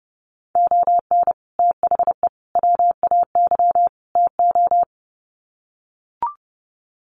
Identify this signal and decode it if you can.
Morse code, heard as a keyed tone, about 30 wpm, 700 Hz. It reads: ON THE WAY TO
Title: Sets of 4 Words 30wpm